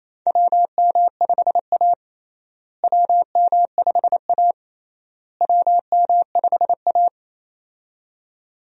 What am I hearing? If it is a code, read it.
Morse code, heard as a keyed tone, about 27 wpm, 700 Hz. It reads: WM5A WM5A WM5A
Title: USA Call Signs - 4 Characters - Repeated 3x - 28wpm